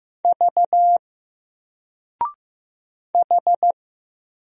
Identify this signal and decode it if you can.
Morse code, heard as a keyed tone, about 15 wpm, 700 Hz. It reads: V H
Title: Single Letter-Number 15wpm